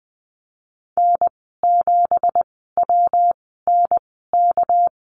Morse code, heard as a keyed tone, about 20 wpm, spaced as for 15 wpm, 700 Hz. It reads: N7WNK
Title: 207 Lesson 33g - Letter j - Call Signs 20wpm